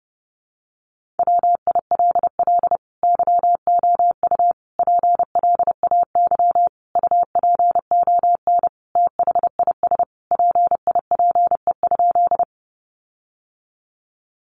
Morse code, heard as a keyed tone, about 29 wpm, 700 Hz. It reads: WILL YOU PLAY UPON THIS PIPE?
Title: Sentences from Top 1500 Words 30wpm